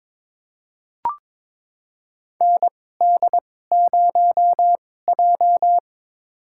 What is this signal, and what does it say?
Morse code, heard as a keyed tone, about 22 wpm, spaced as for 16 wpm, 700 Hz. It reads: ND0J
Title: USA Call Signs - 4 Characters - 2x Character Spacing - 22wpm